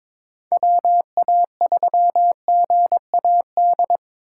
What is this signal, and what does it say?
Morse code, heard as a keyed tone, about 25 wpm, 700 Hz. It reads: WA3GAD
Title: Speed Racing - USA 2x3 Call Signs 22wpm